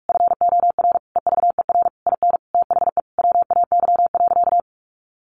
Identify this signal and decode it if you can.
Morse code, heard as a keyed tone, about 47 wpm, 700 Hz. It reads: FOR EVER IN THE WAY.
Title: Sentences from Top 400 Words 45wpm